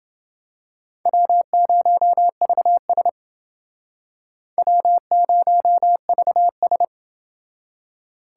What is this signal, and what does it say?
Morse code, heard as a keyed tone, about 28 wpm, 700 Hz. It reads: W0VS W0VS